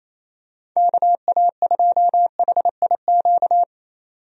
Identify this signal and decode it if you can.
Morse code, heard as a keyed tone, about 27 wpm, 700 Hz. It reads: KA2HIQ
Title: USA Call Signs - 6 Characters - 28wpm